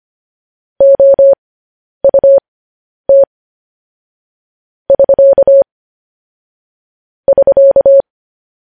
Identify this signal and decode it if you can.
Morse code, heard as a keyed tone, about 25 wpm, spaced as for 10 wpm, 550 Hz. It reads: OUT <SK> <SK>
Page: news-headlines-in-morse-code-at-10-wpm